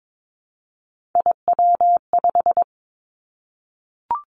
Morse code, heard as a keyed tone, about 22 wpm, 700 Hz. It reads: IW5
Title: Three Challenging Letter-Number Combinations 22wpm